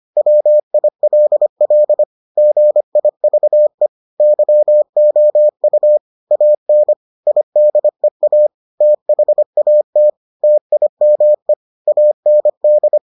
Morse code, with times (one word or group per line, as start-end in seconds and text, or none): 0.2-2.0 WILL
2.4-3.9 GIVE
4.2-6.0 YOU
6.3-6.9 AN
7.3-8.5 IDEA
8.8-10.1 THAT
10.4-11.5 TIME
11.9-13.0 AND